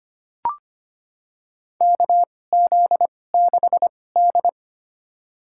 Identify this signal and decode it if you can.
Morse code, heard as a keyed tone, about 25 wpm, spaced as for 18 wpm, 700 Hz. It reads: KZ6D
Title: USA Call Signs - 4 Characters - 2x Character Spacing - 25wpm